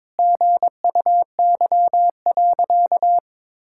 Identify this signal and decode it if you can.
Morse code, heard as a keyed tone, about 22 wpm, 700 Hz. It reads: GUY.
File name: practice / 22wpm / Sentences with 3-Word Phrases - 22wpm